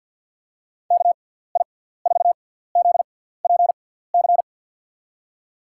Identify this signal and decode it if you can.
Morse code, heard as a keyed tone, about 48 wpm, spaced as for 17 wpm, 700 Hz. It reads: KI4ZPC
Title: USA Call Signs - 6x Character Spacing - 50wpm